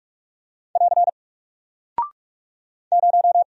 Morse code, heard as a keyed tone, about 44 wpm, 700 Hz. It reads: <AR> 0
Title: Single Character (Letter:Number:Punctuation:Prosign) 45wpm